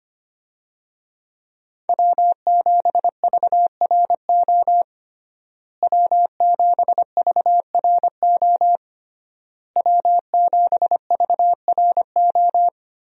Morse code, heard as a keyed tone, about 25 wpm, 700 Hz. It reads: W7VRO W7VRO W7VRO
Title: USA Call Signs - 5 Characters - Repeated 3x - 25wpm